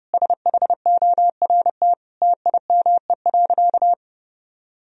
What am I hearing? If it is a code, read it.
Morse code, heard as a keyed tone, about 30 wpm, 700 Hz. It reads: SHORT TIME.
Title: Sentences from Top 1000 Words - Review Part 3 - 30wpm